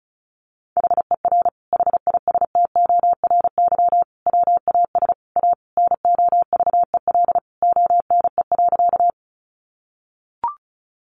Morse code, heard as a keyed tone, about 34 wpm, 700 Hz. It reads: HER HISTORY WAS A NOVEL ONE.
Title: Sentences from Top 1300 Words 35wpm